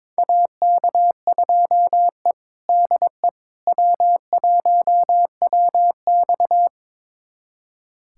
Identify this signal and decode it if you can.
Morse code, heard as a keyed tone, about 22 wpm, 700 Hz. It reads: AK2E DE W1WX
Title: USA Callsign DE Callsign - 4 Characters - 22wpm